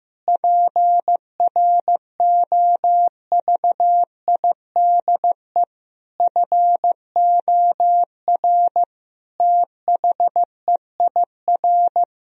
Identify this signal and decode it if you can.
Morse code, heard as a keyed tone, about 15 wpm, 700 Hz. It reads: PROVIDE FOR THEIR
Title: Sentences from Top 1000 Words - Review Part 3 - 15wpm